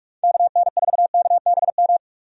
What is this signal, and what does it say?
Morse code, heard as a keyed tone, about 46 wpm, 700 Hz. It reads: KN4KBM
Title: USA Call Signs - 6 Characters - 45wpm